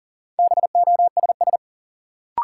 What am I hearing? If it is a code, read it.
Morse code, heard as a keyed tone, about 40 wpm, 700 Hz. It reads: BOSS
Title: Sentences from Top 700 Words - New Word Drill - 40wpm